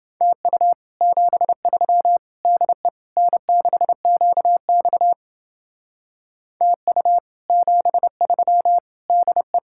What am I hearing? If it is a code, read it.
Morse code, heard as a keyed tone, about 29 wpm, 700 Hz. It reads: TU 73 DE N6QX TU 73 DE